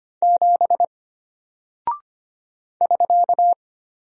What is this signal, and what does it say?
Morse code, heard as a keyed tone, about 25 wpm, 700 Hz. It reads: 7 <SK>
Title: Single Character (Letter:Number:Punctuation:Prosign) Alt 25wpm